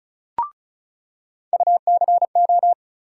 Morse code, heard as a keyed tone, about 36 wpm, 700 Hz. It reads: UCO
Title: USA Call Sign Postfixes - 3 Letters 35wpm